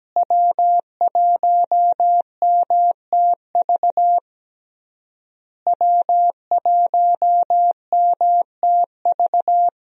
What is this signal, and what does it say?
Morse code, heard as a keyed tone, about 17 wpm, 700 Hz. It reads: W1MTV W1MTV
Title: USA Call Signs - 5 Characters - Repeated 3x - 17wpm